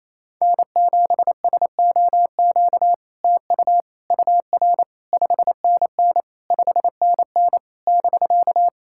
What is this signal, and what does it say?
Morse code, heard as a keyed tone, about 28 wpm, 700 Hz. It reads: N7SOQ TU UR 5NN 5NN <BK>
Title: SOTA QSO Practice 28wpm